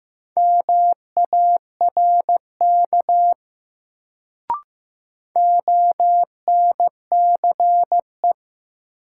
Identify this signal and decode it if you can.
Morse code, heard as a keyed tone, about 15 wpm, 700 Hz. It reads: MARK ONCE
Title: Sentences from Top 300 Words - New Word Drill - 15wpm